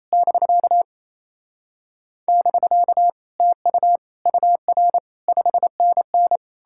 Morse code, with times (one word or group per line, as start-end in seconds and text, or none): 0.1-0.8 <BK>
2.3-3.1 <BK>
3.4-4.0 TU
4.3-5.0 UR
5.3-6.4 5NN